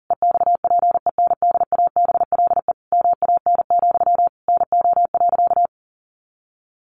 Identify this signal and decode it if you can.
Morse code, heard as a keyed tone, about 40 wpm, 700 Hz. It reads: EXPENDABLE MAN, NO.
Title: A Sound Of Thunder 40wpm